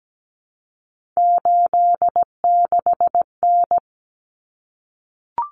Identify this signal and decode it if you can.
Morse code, heard as a keyed tone, about 17 wpm, 700 Hz. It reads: 86N
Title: Common Cut Numbers - 3-Digit 17wpm